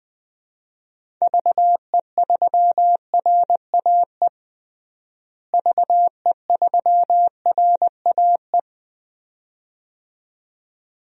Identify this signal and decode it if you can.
Morse code, heard as a keyed tone, about 21 wpm, 700 Hz. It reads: VE3RAE VE3RAE
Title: DXCC-100 Call Signs - 6 Characters - Repeated 2x - 20wpm